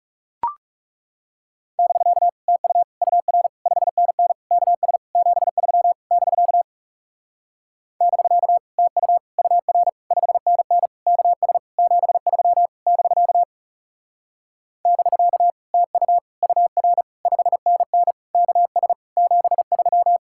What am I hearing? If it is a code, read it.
Morse code, heard as a keyed tone, about 40 wpm, 700 Hz. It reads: <BK> TU UR 5NN KS 73 <BK> <BK> TU UR 5NN KS 73 <BK> <BK> TU UR 5NN KS 73